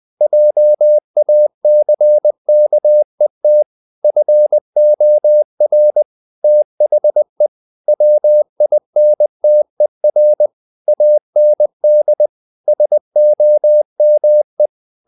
Morse code, heard as a keyed tone, about 20 wpm, 600 Hz. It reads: JACKET FOR THE WINTER AND SOME
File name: the old man and the sea / the old man and the sea part 1 20wpm